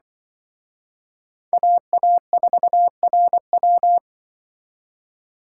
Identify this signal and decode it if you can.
Morse code, heard as a keyed tone, about 24 wpm, 700 Hz. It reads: AA4RW
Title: Speed Racing - USA Call Signs 20wpm